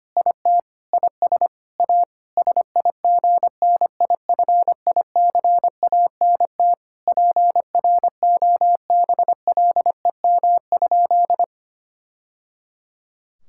Morse code, heard as a keyed tone, about 25 wpm, 700 Hz. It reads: IT IS A SIGNIFICANT PROBLEM?
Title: Sentences from Top 1000 Words - Review Part 6 - 25wpm